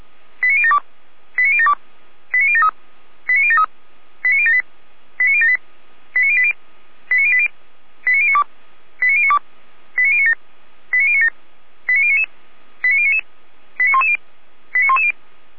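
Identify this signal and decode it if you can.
Five-tone selective call (ZVEI), heard as quick runs of five tones, tones of about 70 ms. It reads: 89072 89072 89073 89073 89078 89078 89080 89080 89092 89092 89097 89097 89099 89099 89110 89110
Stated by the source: ZVEI five-tone call 89050 received on an SDR.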